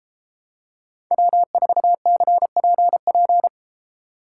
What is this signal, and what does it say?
Morse code, heard as a keyed tone, about 31 wpm, 700 Hz. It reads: W4CPP